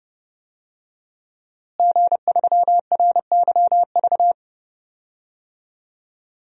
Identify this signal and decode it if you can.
Morse code, heard as a keyed tone, about 30 wpm, 700 Hz. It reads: G3RYV
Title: DXCC-100 Call Signs - 5 Characters - Repeated 2x - 30wpm